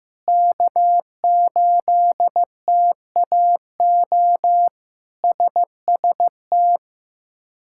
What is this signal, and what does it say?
Morse code, heard as a keyed tone, about 15 wpm, 700 Hz. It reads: K8TAO SST
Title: K1USN SST QSO Practice - 15wpm